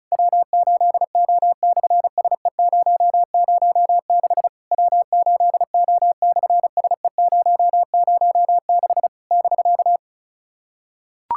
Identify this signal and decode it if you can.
Morse code, heard as a keyed tone, about 33 wpm, 700 Hz. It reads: W8O/SE006 W8O/SE006 <BK>
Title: SOTA QSO Practice 35wpm